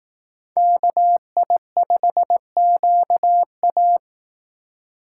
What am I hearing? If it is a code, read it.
Morse code, heard as a keyed tone, about 18 wpm, 700 Hz. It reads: KI5QA
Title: Speed Racing - USA Call Signs 15wpm